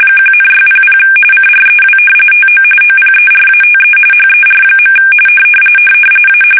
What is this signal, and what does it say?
RTTY, about 45.45 baud. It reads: 1,600 CREW TO OPERATE HER. A
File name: fsk850 / battleshipnj